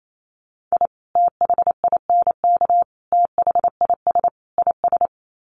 Morse code, heard as a keyed tone, about 28 wpm, 700 Hz. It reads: I THINK THIS IS
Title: Sets of 4 Words 28wpm